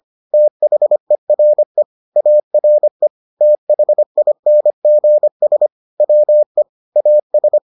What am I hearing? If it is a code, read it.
Morse code, heard as a keyed tone, about 25 wpm, 600 Hz. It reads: THERE ARE THINGS WE AS